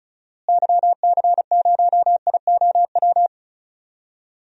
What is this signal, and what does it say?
Morse code, heard as a keyed tone, about 36 wpm, 700 Hz. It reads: YC0IOW
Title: DXCC-100 Call Signs - 6 Characters - Repeated 3x - 35wpm